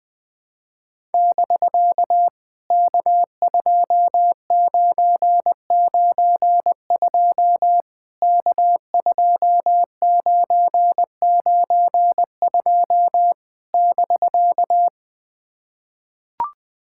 Morse code, heard as a keyed tone, about 19 wpm, 700 Hz. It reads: <BK> K2992 K2992 <BK>
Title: POTA QSO Practice - 20wpm